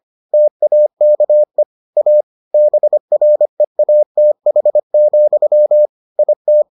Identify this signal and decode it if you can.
Morse code, heard as a keyed tone, about 25 wpm, 600 Hz. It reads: TAKE A BREATH, IT